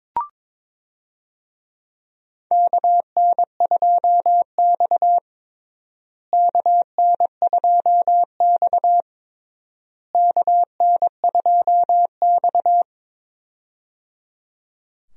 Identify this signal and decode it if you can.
Morse code, heard as a keyed tone, about 22 wpm, 700 Hz. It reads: KN2X KN2X KN2X
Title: USA Call Signs - 4 Characters - Repeated 3x - 22wpm